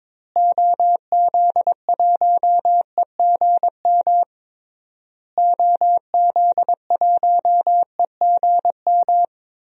Morse code, heard as a keyed tone, about 21 wpm, 700 Hz. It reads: OZ1EGM OZ1EGM
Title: DXCC-100 Call Signs - 6 Characters - Repeated 3x - 22wpm